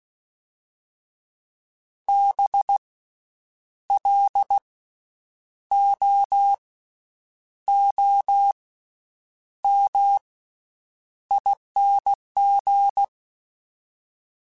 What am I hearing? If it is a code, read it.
Morse code, heard as a keyed tone, about 16 wpm, spaced as for 8 wpm, 800 Hz. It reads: BLOOMING
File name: Special INGs-mp3s / Special INGs-mp3s / Slow ing @15